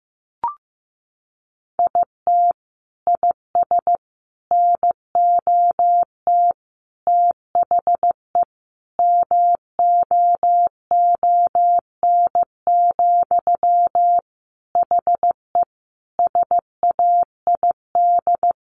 Morse code, heard as a keyed tone, about 15 wpm, 700 Hz. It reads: IT IS NOT THE MOON, HE SAID
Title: Sentences from Top 1100 Words 15wpm